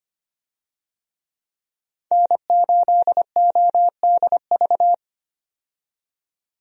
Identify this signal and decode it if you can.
Morse code, heard as a keyed tone, about 25 wpm, 700 Hz. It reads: N8ODV
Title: USA Call Signs - 5 Characters - 25wpm